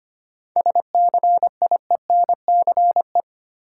Morse code, heard as a keyed tone, about 25 wpm, 700 Hz. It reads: SCIENCE
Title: Sentences from Top 1000 Words - Review Part 2 - New Word Drill - 25wpm